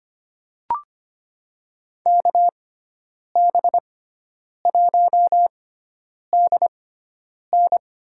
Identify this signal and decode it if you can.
Morse code, heard as a keyed tone, about 25 wpm, spaced as for 9 wpm, 700 Hz. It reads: KB1DN
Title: USA Call Signs - 6x Character Spacing - 25wpm